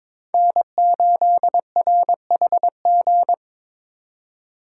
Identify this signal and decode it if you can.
Morse code, heard as a keyed tone, about 22 wpm, 700 Hz. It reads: N8RHG
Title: USA Call Signs - 8x Character Spacing - 22wpm